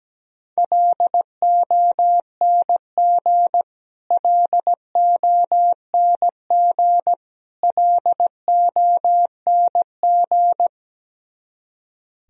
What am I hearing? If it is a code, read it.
Morse code, heard as a keyed tone, about 17 wpm, 700 Hz. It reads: LONG LONG LONG
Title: Top 100 Words Repeated 17wpm